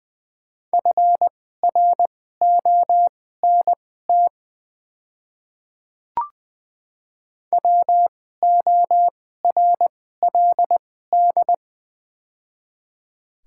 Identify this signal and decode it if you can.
Morse code, heard as a keyed tone, about 20 wpm, spaced as for 14 wpm, 700 Hz. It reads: FRONT WORLD
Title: 2x Character Spacing - Top 500 Words - 20wpm